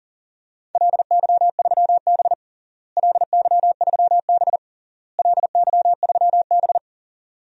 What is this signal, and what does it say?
Morse code, heard as a keyed tone, about 40 wpm, 700 Hz. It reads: LY3B LY3B LY3B
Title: DXCC-100 Call Signs - 4 Characters - Repeated 3x - 40wpm